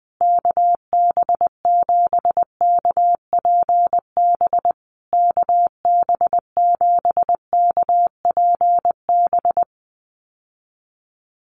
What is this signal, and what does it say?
Morse code, heard as a keyed tone, about 20 wpm, 700 Hz. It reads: KB7KPB KB7KPB
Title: USA Call Signs Repeated 3x 20wpm